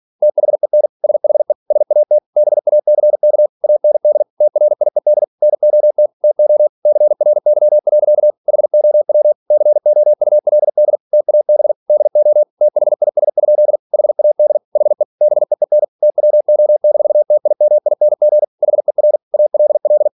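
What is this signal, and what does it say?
Morse code, heard as a keyed tone, about 46 wpm, 600 Hz. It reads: THEN SHE SAT BACK AND TRIED NOT TO CRY. HOW COULD TAB DO THIS? HAD HE BEEN TWO-TIMING HER ALL
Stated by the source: Practice text from an English novel sent at 47 WPM.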